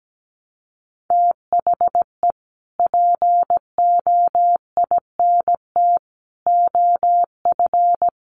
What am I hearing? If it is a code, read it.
Morse code, heard as a keyed tone, about 17 wpm, 700 Hz. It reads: THE POINT OF
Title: Sets of 5 Words 17wpm